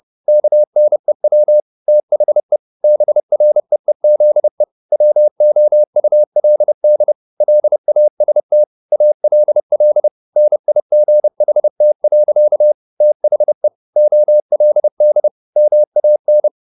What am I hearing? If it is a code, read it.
Morse code, heard as a keyed tone, about 30 wpm, 600 Hz. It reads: KNEW THE BREEZE WOULD LAST ALL NIGHT. THE OLD MAN